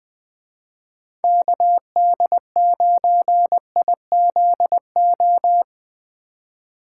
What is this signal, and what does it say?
Morse code, heard as a keyed tone, about 23 wpm, 700 Hz. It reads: KD9IZO
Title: Speed Racing - USA 2x3 Call Signs 20wpm